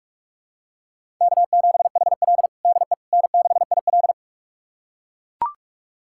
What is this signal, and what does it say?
Morse code, heard as a keyed tone, about 43 wpm, 700 Hz. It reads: K7HL DE N6IL
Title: USA Callsign DE Callsign - 4 Characters - 45wpm